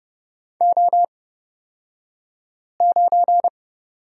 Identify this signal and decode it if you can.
Morse code, heard as a keyed tone, about 31 wpm, spaced as for 5 wpm, 700 Hz. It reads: O9
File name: Lesson 36d - BT - Two Characters 30wpm